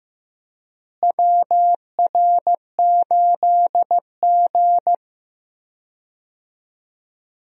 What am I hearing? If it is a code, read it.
Morse code, heard as a keyed tone, about 15 wpm, 700 Hz. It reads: WR8G
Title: USA Call Signs - 4 Characters - Repeated 2x - 15wpm